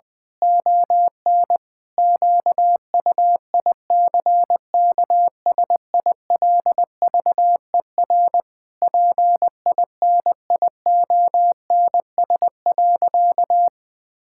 Morse code, heard as a keyed tone, about 20 wpm, 700 Hz. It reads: ON QUICKSILVER PINIONS.
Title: Pangram Challenge v2 - 20wpm